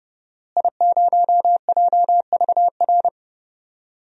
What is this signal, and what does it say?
Morse code, heard as a keyed tone, about 30 wpm, 700 Hz. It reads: I0JVR
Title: DXCC-100 Call Signs - 5 Characters - Repeated 3x - 30wpm